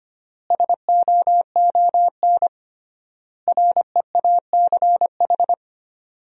Morse code, heard as a keyed tone, about 25 wpm, 700 Hz. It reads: SOON REACH
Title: 3x Word-Spacing - Sentences from Top 400 Words - 25wpm